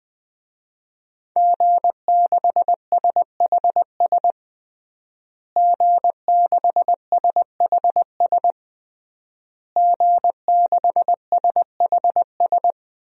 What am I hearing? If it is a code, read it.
Morse code, heard as a keyed tone, about 20 wpm, 700 Hz. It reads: G6SHS G6SHS G6SHS
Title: DXCC-100 Call Signs - 5 Characters - Repeated 3x - 20wpm